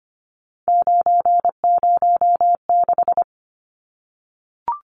Morse code, heard as a keyed tone, about 25 wpm, 700 Hz. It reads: 906
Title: Numbers - 3-Digit 25wpm